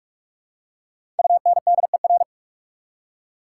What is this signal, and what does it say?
Morse code, heard as a keyed tone, about 42 wpm, 700 Hz. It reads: UNDER
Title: Prefixes and Suffixes 45wpm